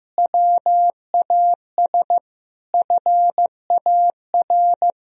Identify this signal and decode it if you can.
Morse code, heard as a keyed tone, about 16 wpm, 700 Hz. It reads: WAS FAR
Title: Sentences from Top 1000 Words - Review Part 3 - 15wpm